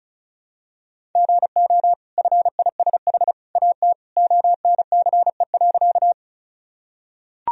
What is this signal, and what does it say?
Morse code, heard as a keyed tone, about 36 wpm, 700 Hz. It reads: GO FISH AT ONCE.
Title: Sentences from Top 1000 Words - Review Part 1 - 35wpm